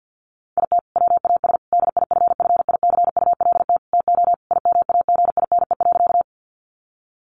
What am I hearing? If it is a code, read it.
Morse code, heard as a keyed tone, about 50 wpm, 700 Hz. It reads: IT WAS DIFFICULT TO IMAGINE.
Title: Sentences from Top 1400 Words 50wpm